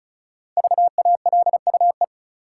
Morse code, heard as a keyed tone, about 36 wpm, 700 Hz. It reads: VALUE
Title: Sentences from Top 600 Words - New Word Drill - 35wpm